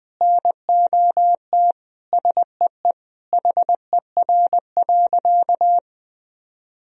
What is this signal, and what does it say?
Morse code, heard as a keyed tone, about 20 wpm, 700 Hz. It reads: NOT SEE HER.
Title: Sentences from Top 1000 Words - Review Part 4 - 20wpm